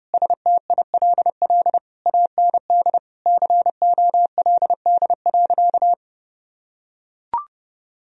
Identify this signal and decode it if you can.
Morse code, heard as a keyed tone, about 32 wpm, 700 Hz. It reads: STILL AND COLD.
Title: Sentences from Top 1000 Words - Review Part 2 - 30wpm